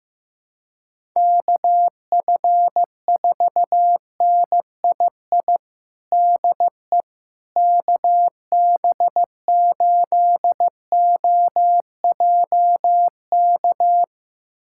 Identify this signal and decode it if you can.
Morse code, heard as a keyed tone, about 15 wpm, 700 Hz. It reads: KF4NII DE KB8OJK
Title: USA Callsign DE Callsign - 6 Characters - 15wpm